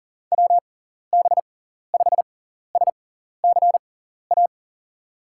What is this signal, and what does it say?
Morse code, heard as a keyed tone, about 40 wpm, spaced as for 14 wpm, 700 Hz. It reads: WB5SCA
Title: USA Call Signs - 6 Characters - 6x Character Spacing - 40wpm